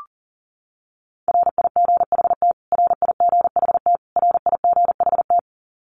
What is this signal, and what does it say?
Morse code, heard as a keyed tone, about 40 wpm, 700 Hz. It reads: RIGHT RIGHT RIGHT
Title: Top 500 Words Repeated 40wpm